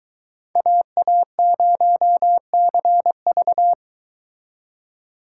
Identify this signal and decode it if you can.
Morse code, heard as a keyed tone, about 23 wpm, 700 Hz. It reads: AA0CV